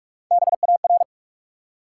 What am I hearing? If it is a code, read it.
Morse code, heard as a keyed tone, about 44 wpm, 700 Hz. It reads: BAR
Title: Three Letter Words - 45wpm